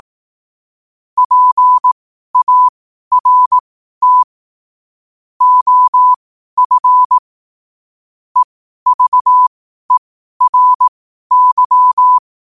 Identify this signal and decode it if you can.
Morse code, heard as a keyed tone, about 18 wpm, spaced as for 12 wpm, 1000 Hz. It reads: PART OF EVERY
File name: 12wpm